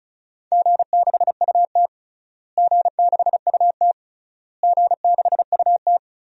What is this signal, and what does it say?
Morse code, heard as a keyed tone, about 36 wpm, 700 Hz. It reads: G6UT G6UT G6UT
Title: DXCC-100 Call Signs - 4 Characters - Repeated 3x - 35wpm